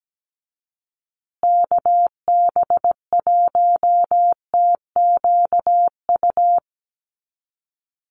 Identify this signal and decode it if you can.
Morse code, heard as a keyed tone, about 17 wpm, 700 Hz. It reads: KB1TQU
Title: USA Call Signs Repeated 2x 17wpm